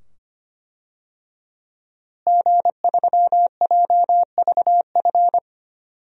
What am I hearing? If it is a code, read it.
Morse code, heard as a keyed tone, about 25 wpm, 700 Hz. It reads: G3JVF
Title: DXCC-100 Call Signs - 5 Characters - Repeated 2x - 25wpm